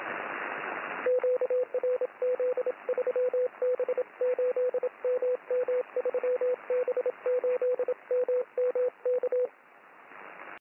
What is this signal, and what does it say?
Morse code, heard as a keyed tone, about 27 wpm, 500 Hz, with radio noise. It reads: QRZ 3B8MM 3B8MM K
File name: boos / 3b8mm-6m-cw